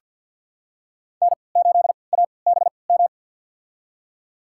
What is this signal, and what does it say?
Morse code, heard as a keyed tone, about 50 wpm, spaced as for 28 wpm, 700 Hz. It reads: N8ABM
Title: USA Call Signs - 5 Characters - 3x Character Spacing - 50wpm